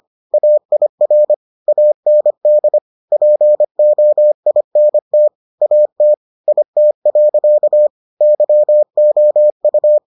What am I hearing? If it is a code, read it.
Morse code, heard as a keyed tone, about 25 wpm, 600 Hz. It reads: AIR AND POINT AT IT. YOU